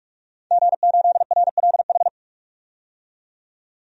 Morse code, heard as a keyed tone, about 42 wpm, 700 Hz. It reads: G8RLH
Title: DXCC-100 Call Signs - 5 Characters - 45wpm